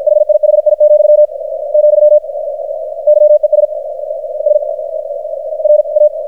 Morse code, heard as a keyed tone, about 53 wpm, 600 Hz, with radio noise. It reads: HIHI, , YES S R A
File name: archived / CRSnet